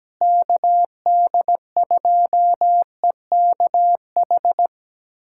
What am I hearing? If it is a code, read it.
Morse code, heard as a keyed tone, about 17 wpm, 700 Hz. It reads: KD2EKH
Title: USA Callsign DE Callsign 17wpm